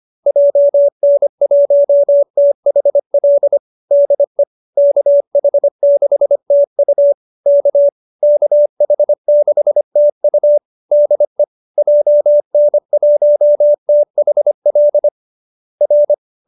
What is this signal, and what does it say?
Morse code, heard as a keyed tone, about 25 wpm, 570 Hz. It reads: JN1THL DE KH6TU K KH6TU DE JN1THL R